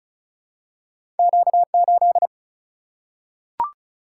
Morse code, heard as a keyed tone, about 36 wpm, 700 Hz. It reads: Q 8
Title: Single-Letter Single-Number 35wpm